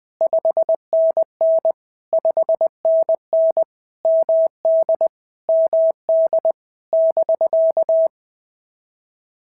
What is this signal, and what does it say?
Morse code, heard as a keyed tone, about 19 wpm, 650 Hz. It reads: 5NN 5NN MD MD <BK>
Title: POTA QSO Practice - 20wpm